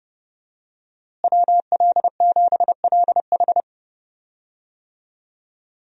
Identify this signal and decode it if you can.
Morse code, heard as a keyed tone, about 32 wpm, 700 Hz. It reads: WL7LH
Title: USA Call Signs - 5 Characters - Repeated 2x - 30wpm